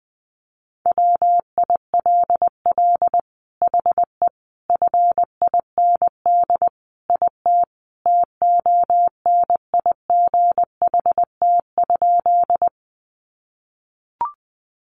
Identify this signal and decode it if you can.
Morse code, heard as a keyed tone, about 20 wpm, 700 Hz. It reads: WILL HE FIND IT TONIGHT?
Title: Sentences from Top 600 Words 20wpm